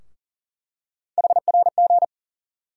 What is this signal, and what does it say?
Morse code, heard as a keyed tone, about 40 wpm, 700 Hz. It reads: HRG
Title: USA Call Sign Postfixes - 3 Letters 40wpm